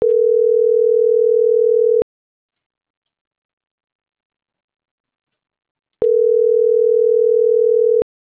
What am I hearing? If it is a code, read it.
Ringback tone (North America), 440 + 480 Hz.